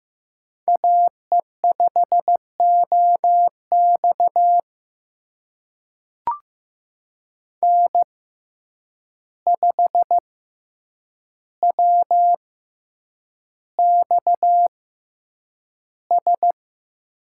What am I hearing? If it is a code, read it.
Morse code, heard as a keyed tone, about 15 wpm, spaced as for 8 wpm, 700 Hz. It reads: AE5OX N5WXS
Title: USA Call Signs - 5 Characters - 6x Character Spacing - 15wpm